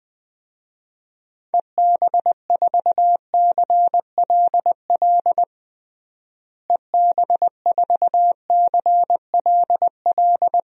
Morse code, heard as a keyed tone, about 20 wpm, 700 Hz. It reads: EB4CLL EB4CLL
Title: DXCC-100 Call Signs - 6 Characters - Repeated 3x - 20wpm